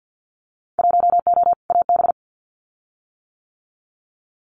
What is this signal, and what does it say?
Morse code, heard as a keyed tone, about 50 wpm, 700 Hz. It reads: 1O AB
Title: Winter Field Day Exchanges Encore 50wpm